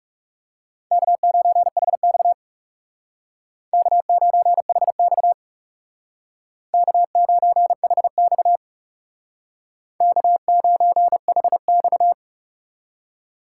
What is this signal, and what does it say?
Morse code, heard as a keyed tone, about 38 wpm, 700 Hz. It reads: K9HX K9HX K9HX K9HX